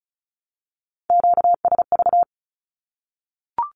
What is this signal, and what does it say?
Morse code, heard as a keyed tone, about 34 wpm, 700 Hz. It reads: QSV
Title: Q-Codes 35wpm